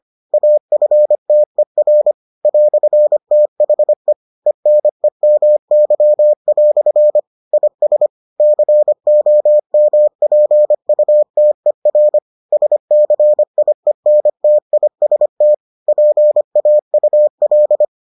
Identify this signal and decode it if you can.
Morse code, heard as a keyed tone, about 25 wpm, 600 Hz. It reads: AFTER "THE ENEMY" IS COMPUTER SCIENTIST PAUL